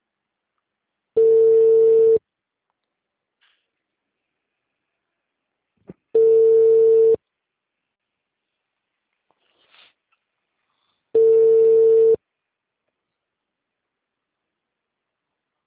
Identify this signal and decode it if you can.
Ringback tone (China, 450 Hz).
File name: out- -2016